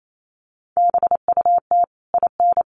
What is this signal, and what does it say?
Morse code, heard as a keyed tone, about 28 wpm, 700 Hz. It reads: BUT IN
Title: Sentences from Top 1500 Words 28wpm